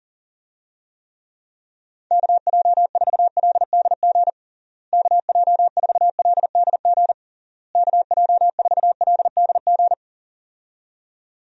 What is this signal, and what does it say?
Morse code, heard as a keyed tone, about 40 wpm, 700 Hz. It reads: KJ4LDG KJ4LDG KJ4LDG
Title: USA Call Signs - 6 Characters - Repeated 3x - 40wpm